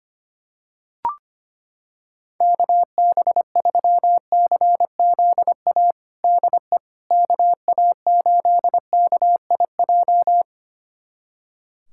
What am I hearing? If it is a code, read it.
Morse code, heard as a keyed tone, about 25 wpm, 700 Hz. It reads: KB3CZA DE KA8KIJ
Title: USA Callsign DE Callsign - 6 Characters - 25wpm